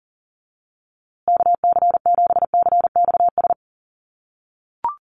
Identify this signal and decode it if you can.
Morse code, heard as a keyed tone, about 40 wpm, 700 Hz. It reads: KC7CXS
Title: USA Call Signs Repeated 3x 40wpm